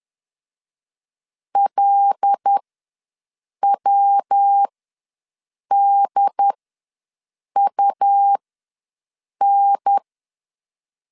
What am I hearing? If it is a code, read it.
Morse code, heard as a keyed tone, about 10 wpm, spaced as for 6 wpm, 800 Hz. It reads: LWDUN
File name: LEZIONI / 5 LBVC / 5 LBV ws 10wpm 3645